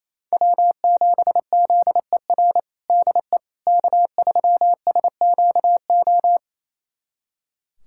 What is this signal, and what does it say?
Morse code, heard as a keyed tone, about 28 wpm, 700 Hz. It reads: W7ZER DE K3SQO
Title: USA Callsign DE Callsign - 5 Characters - 28wpm